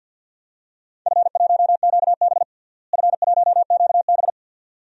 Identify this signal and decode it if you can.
Morse code, heard as a keyed tone, about 53 wpm, 700 Hz. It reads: F1QB F1QB
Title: DXCC-100 Call Signs - 4 Characters - Repeated 2x - 50wpm